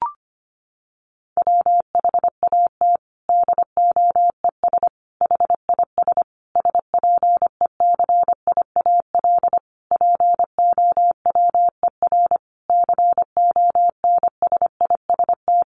Morse code, heard as a keyed tone, about 25 wpm, 700 Hz. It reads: WHAT DOES HIS SPECIAL POWER CONSIST
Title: Sentences from Top 1400 Words 25wpm